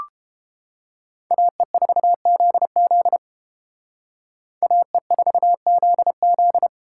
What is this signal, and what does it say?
Morse code, heard as a keyed tone, about 30 wpm, 700 Hz. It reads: AE4ZZ AE4ZZ